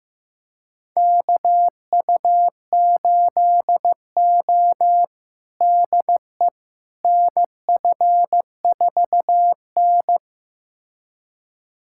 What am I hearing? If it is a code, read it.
Morse code, heard as a keyed tone, about 15 wpm, 700 Hz. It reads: KU8O DE NF4N
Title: USA Callsign DE Callsign - 4 Characters - 15wpm